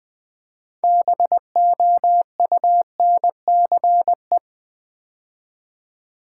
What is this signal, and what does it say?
Morse code, heard as a keyed tone, about 20 wpm, 690 Hz. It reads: BOUNCE
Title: Build A Word - 2 Letters At A Time - 6-Letter Words - 20wpm